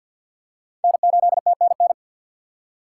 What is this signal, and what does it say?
Morse code, heard as a keyed tone, about 46 wpm, 700 Hz. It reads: N8TNN
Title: USA Call Signs - 5 Characters - 4x Character Spacing - 50wpm